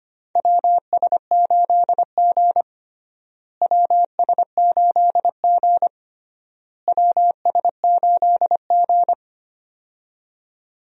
Morse code, heard as a keyed tone, about 25 wpm, 700 Hz. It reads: WS8G WS8G WS8G
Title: USA Call Signs - 4 Characters - Repeated 3x - 25wpm